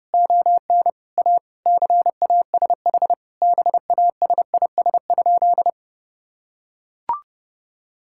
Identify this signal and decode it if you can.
Morse code, heard as a keyed tone, about 32 wpm, 700 Hz. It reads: ON A CASH BASIS?
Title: Sentences from Top 1000 Words - Review Part 5 - 30wpm